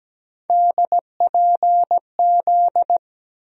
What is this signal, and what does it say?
Morse code, heard as a keyed tone, about 17 wpm, 700 Hz. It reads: DPZ
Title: USA Call Sign Postfixes - 3 Letters 17wpm